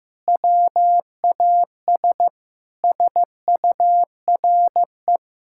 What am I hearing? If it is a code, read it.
Morse code, heard as a keyed tone, about 15 wpm, 700 Hz. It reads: WAS SURE
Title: Sentences from Top 1600 Words 15wpm